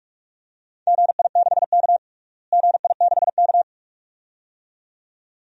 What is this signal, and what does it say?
Morse code, heard as a keyed tone, about 43 wpm, 700 Hz. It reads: GI6K GI6K
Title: DXCC-100 Call Signs - 4 Characters - Repeated 2x - 45wpm